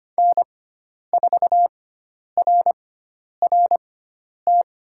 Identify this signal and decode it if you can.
Morse code, heard as a keyed tone, about 25 wpm, spaced as for 10 wpm, 700 Hz. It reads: N4RRT
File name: Lesson 16g - Letter c - Call Signs 25wpm